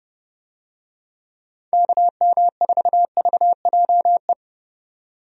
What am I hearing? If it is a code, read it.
Morse code, heard as a keyed tone, about 30 wpm, 700 Hz. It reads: KM4VJE
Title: USA Call Signs - 6 Characters - 30wpm